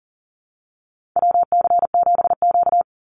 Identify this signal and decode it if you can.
Morse code, heard as a keyed tone, about 40 wpm, 700 Hz. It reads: WC7Q
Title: Call Signs - 40wpm